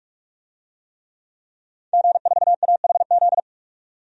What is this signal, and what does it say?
Morse code, heard as a keyed tone, about 43 wpm, 700 Hz. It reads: G4AHZ
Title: DXCC-100 Call Signs - 5 Characters - 45wpm